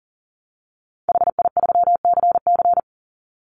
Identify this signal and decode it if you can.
Morse code, heard as a keyed tone, about 40 wpm, 700 Hz. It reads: HI3CC
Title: Call Signs Repeated 3x - 40wpm